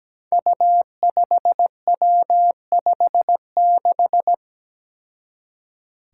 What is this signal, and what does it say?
Morse code, heard as a keyed tone, about 17 wpm, 700 Hz. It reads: U5W56
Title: Five Challenging Letter-Number Combinations 17wpm